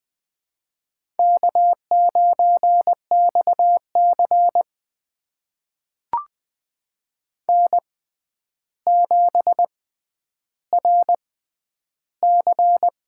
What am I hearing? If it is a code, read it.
Morse code, heard as a keyed tone, about 20 wpm, spaced as for 10 wpm, 700 Hz. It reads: K9XC N7RC
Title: USA Call Signs - 4 Characters - 6x Character Spacing - 20wpm